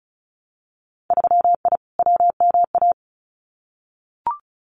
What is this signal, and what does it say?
Morse code, heard as a keyed tone, about 36 wpm, 700 Hz. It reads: 3I WMA
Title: Winter Field Day Exchanges Encore 35wpm